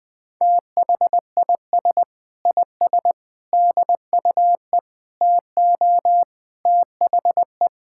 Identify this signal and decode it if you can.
Morse code, heard as a keyed tone, about 20 wpm, 700 Hz. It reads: THIS IS DUE TO THE
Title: Sentences with 7-Word Phrases - 20wpm